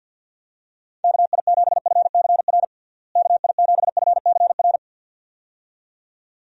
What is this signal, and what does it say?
Morse code, heard as a keyed tone, about 51 wpm, 700 Hz. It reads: KI7FCR KI7FCR
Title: USA Call Signs - 6 Characters - Repeated 2x - 50wpm